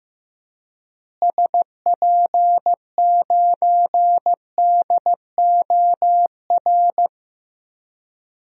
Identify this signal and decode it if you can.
Morse code, heard as a keyed tone, about 15 wpm, 700 Hz. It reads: SP9DOR
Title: DXCC-100 Call Signs - 6 Characters - Repeated 3x - 15wpm